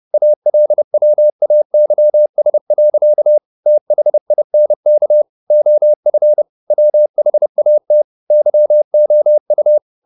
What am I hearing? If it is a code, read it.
Morse code, heard as a keyed tone, about 30 wpm, 600 Hz. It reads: ALWAYS. THINK OF WHAT YOU
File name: the old man and the sea / the old man and the sea 1 30wpm